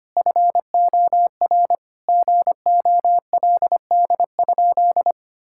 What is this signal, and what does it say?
Morse code, heard as a keyed tone, about 25 wpm, 700 Hz. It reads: FOR GOLD?
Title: Sentences from Top 1600 Words 25wpm